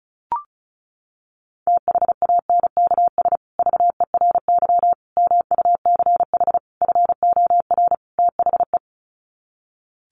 Morse code, heard as a keyed tone, about 34 wpm, 700 Hz. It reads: THANKS VERY MUCH FOR THE
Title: Sets of 5 Words Encore2 35wpm